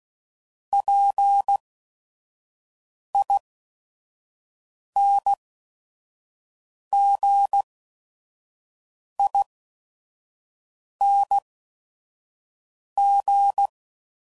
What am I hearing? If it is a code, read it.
Morse code, heard as a keyed tone, about 16 wpm, spaced as for 5 wpm, 800 Hz. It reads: PINGING